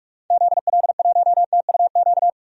Morse code, heard as a keyed tone, about 47 wpm, 700 Hz. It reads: ZL1TUQ
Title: DXCC-100 Call Signs - 6 Characters - 45wpm